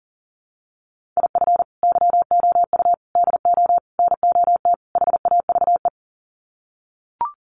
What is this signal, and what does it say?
Morse code, heard as a keyed tone, about 40 wpm, 700 Hz. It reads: IF YOU DO NOT HAVE
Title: Sets of 5 Words 40wpm